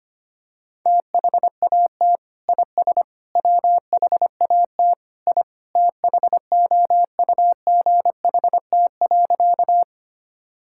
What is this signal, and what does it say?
Morse code, heard as a keyed tone, about 25 wpm, 700 Hz. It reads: THAT IS WHAT I THOUGHT.
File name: practice / 25wpm / Sentences with 2-Word Phrases - 25wpm